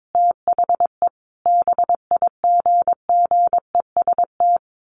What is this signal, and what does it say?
Morse code, heard as a keyed tone, about 22 wpm, 700 Hz. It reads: THE BIGGEST
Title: A Sound Of Thunder 22wpm